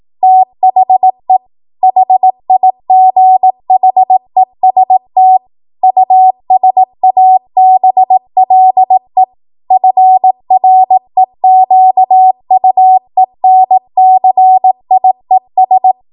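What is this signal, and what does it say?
Morse code, heard as a keyed tone, about 18 wpm, 750 Hz. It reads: THE HIGHEST USABLE FREQUENCIES
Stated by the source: 18 WPM Morse code practice from a national amateur radio association's code archive.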